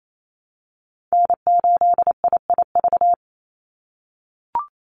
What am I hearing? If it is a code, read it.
Morse code, heard as a keyed tone, about 28 wpm, 700 Hz. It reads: N8IIV
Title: USA Call Signs 28wpm